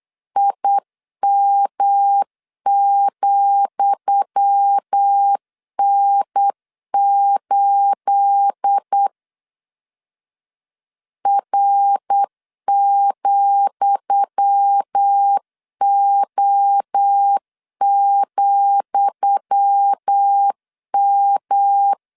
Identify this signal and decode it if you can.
Morse code, heard as a keyed tone, about 8 wpm, 800 Hz. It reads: IM,N8 R,O,M
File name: LEZIONI / 10 SIGNS / 10D COMA 8wpm 3749